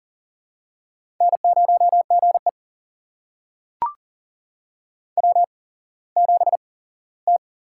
Morse code, heard as a keyed tone, about 40 wpm, spaced as for 19 wpm, 690 Hz. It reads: N0GE W7T
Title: USA Call Signs - 8x Character Spacing - 40wpm